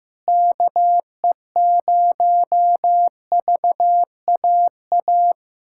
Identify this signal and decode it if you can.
Morse code, heard as a keyed tone, about 15 wpm, 700 Hz. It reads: KE0VAA
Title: USA Callsign DE Callsign - 6 Characters - 15wpm